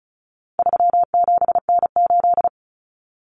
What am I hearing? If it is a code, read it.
Morse code, heard as a keyed tone, about 34 wpm, 700 Hz. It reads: 37N8
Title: Common Cut Numbers - 4-Digit 35wpm